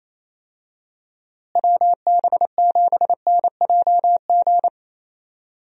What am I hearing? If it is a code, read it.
Morse code, heard as a keyed tone, about 28 wpm, 700 Hz. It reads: WB7NJG
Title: USA Call Signs - 6 Characters - Repeated 3x - 28wpm